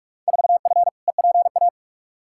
Morse code, heard as a keyed tone, about 42 wpm, 700 Hz. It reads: 4F EPA
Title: Field Day Exchanges Encore - 45wpm